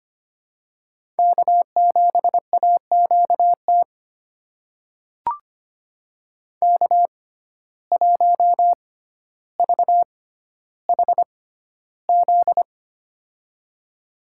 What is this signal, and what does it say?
Morse code, heard as a keyed tone, about 25 wpm, spaced as for 13 wpm, 700 Hz. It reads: K7AQT K1VHZ
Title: USA Call Signs - 5 Characters - 6x Character Spacing - 25wpm